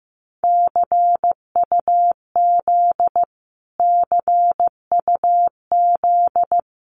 Morse code, heard as a keyed tone, about 15 wpm, 700 Hz. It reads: CUZ CUZ
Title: Abbreviations Repeated 3x - 15wpm